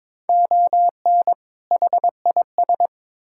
Morse code, heard as a keyed tone, about 22 wpm, 700 Hz. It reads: ON HIS
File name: Build A Phrase - 2-Word Phrases - 22wpm